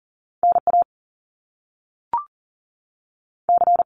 Morse code, heard as a keyed tone, about 40 wpm, 700 Hz. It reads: NA C
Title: Top Two Letter Combinations in Words 40wpm